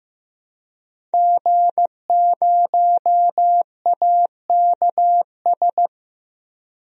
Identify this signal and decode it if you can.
Morse code, heard as a keyed tone, about 15 wpm, 700 Hz. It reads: G0AKS
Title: DXCC-100 Call Signs - 5 Characters - Repeated 2x - 15wpm